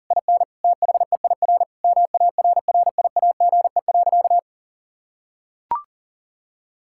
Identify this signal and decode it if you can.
Morse code, heard as a keyed tone, about 40 wpm, 700 Hz. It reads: IN THEIR MARRIAGE.
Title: Sentences from Top 1000 Words - Review Part 6 - 40wpm